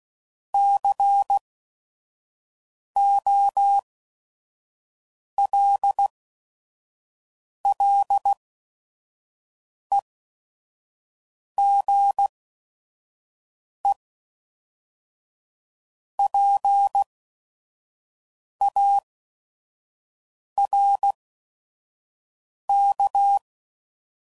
Morse code, heard as a keyed tone, about 16 wpm, spaced as for 5 wpm, 800 Hz. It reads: COLLEGEPARK
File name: CW Code Groups @5WPM / fromSM's / Cities